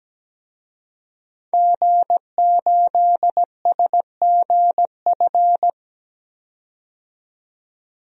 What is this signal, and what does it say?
Morse code, heard as a keyed tone, about 17 wpm, 700 Hz. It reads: G8SGF
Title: DXCC-100 Call Signs - 5 Characters - 17wpm